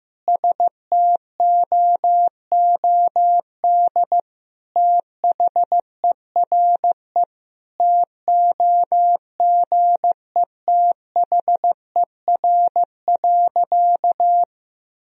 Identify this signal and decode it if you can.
Morse code, heard as a keyed tone, about 16 wpm, 700 Hz. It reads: STOOD THERE TOGETHER.
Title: Sentences from Top 1000 Words - Review Part 2 - 15wpm